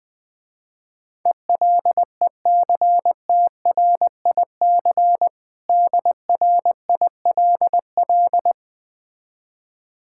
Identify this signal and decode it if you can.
Morse code, heard as a keyed tone, about 20 wpm, 700 Hz. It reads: ELECTRIC DRILL
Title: Difficult Words with Hints - 20wpm